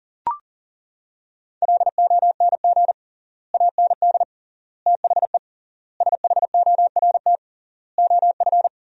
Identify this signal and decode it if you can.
Morse code, heard as a keyed tone, about 40 wpm, 700 Hz. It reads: LONG AND THE SHORT OF
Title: 3x Word-Spacing - Sentences from Top 400 Words - 40wpm